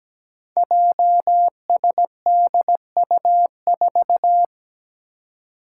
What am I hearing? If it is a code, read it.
Morse code, heard as a keyed tone, about 17 wpm, 700 Hz. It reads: JSDU4
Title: Five Challenging Letter-Number Combinations 17wpm